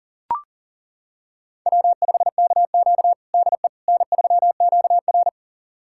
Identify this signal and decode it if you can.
Morse code, heard as a keyed tone, about 40 wpm, 700 Hz. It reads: W5KQ DE N3QR
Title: USA Callsign DE Callsign - 4 Characters - 40wpm